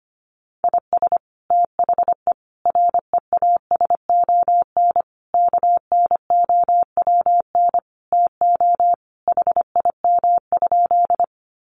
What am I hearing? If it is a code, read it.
Morse code, heard as a keyed tone, about 25 wpm, 700 Hz. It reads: IS THE REASON KNOWN TO HIM?
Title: Sentences from Top 700 Words 25wpm